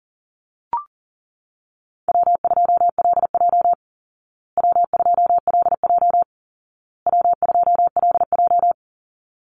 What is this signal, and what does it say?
Morse code, heard as a keyed tone, about 40 wpm, 700 Hz. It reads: W2LJ W2LJ W2LJ
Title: Easy Call Signs Repeated 3x - 40wpm